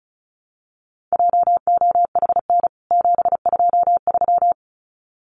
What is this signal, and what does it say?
Morse code, heard as a keyed tone, about 34 wpm, 700 Hz. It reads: JOHN 723
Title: CWT Exchange 35wpm